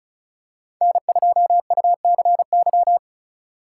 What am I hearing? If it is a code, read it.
Morse code, heard as a keyed tone, about 34 wpm, 700 Hz. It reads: N2UCY
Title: USA Call Signs - 5 Characters - 3x Character Spacing - 35wpm